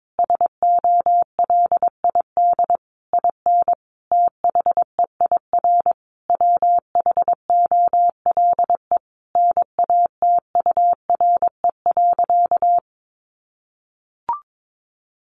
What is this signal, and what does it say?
Morse code, heard as a keyed tone, about 22 wpm, 700 Hz. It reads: SOLID IN THEIR WHOLE NATURE.
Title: Sentences from Top 1500 Words 22wpm